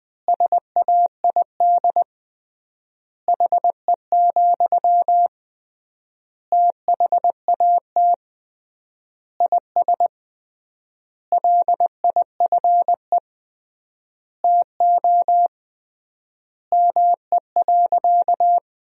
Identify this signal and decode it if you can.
Morse code, heard as a keyed tone, about 21 wpm, 700 Hz. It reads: SAID HE, THAT IS LIFE TO ME.
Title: 3x Word-Spacing - Sentences from Top 500 Words - 20wpm